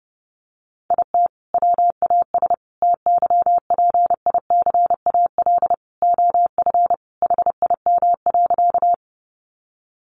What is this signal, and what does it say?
Morse code, heard as a keyed tone, about 30 wpm, 700 Hz. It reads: IT WAS TYPICAL OF HIM.
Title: Sentences from Top 1100 Words 30wpm